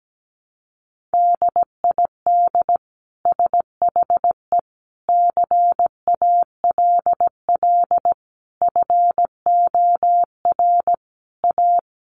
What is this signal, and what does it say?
Morse code, heard as a keyed tone, about 17 wpm, 700 Hz. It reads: DID SHE CALL FOR A
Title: Sentences from Top 1300 Words 17wpm